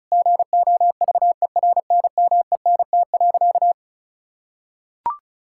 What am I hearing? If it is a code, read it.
Morse code, heard as a keyed tone, about 36 wpm, 700 Hz. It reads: GOVERNMENT.
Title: Sentences from Top 1000 Words - Review Part 5 - 35wpm